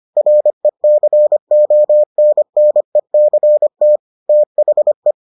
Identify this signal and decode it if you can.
Morse code, heard as a keyed tone, about 25 wpm, 600 Hz. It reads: RECONNECT THE